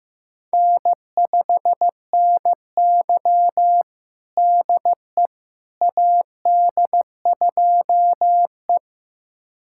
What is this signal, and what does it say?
Morse code, heard as a keyed tone, about 15 wpm, 700 Hz. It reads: N5NY DE AD2E
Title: USA Callsign DE Callsign - 4 Characters - 15wpm